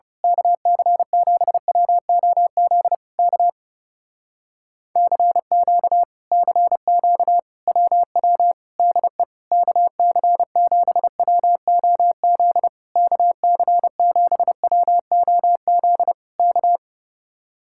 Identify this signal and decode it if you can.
Morse code, heard as a keyed tone, about 30 wpm, 700 Hz. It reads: KC7WOZ K CQ CQ WW DE KC7WOZ KC7WOZ K
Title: Speed-Racing Common QSO Phrases - 30wpm